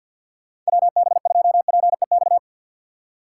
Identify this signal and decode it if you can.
Morse code, heard as a keyed tone, about 61 wpm, 700 Hz. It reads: WB2PEX
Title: Speed Racing - USA 2x3 Call Signs 50wpm